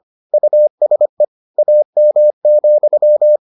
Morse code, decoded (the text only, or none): USE AM,